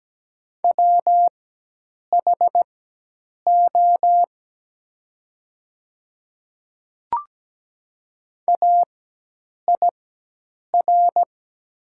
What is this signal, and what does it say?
Morse code, heard as a keyed tone, about 17 wpm, spaced as for 8 wpm, 700 Hz. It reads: WHO AIR
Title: 4x Character Spacing - Top 200 Words - 17wpm